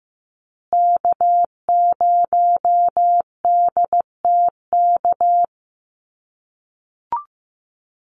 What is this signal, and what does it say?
Morse code, heard as a keyed tone, about 15 wpm, 700 Hz. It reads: K0DTK
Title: USA Call Signs Repeated 3x 15wpm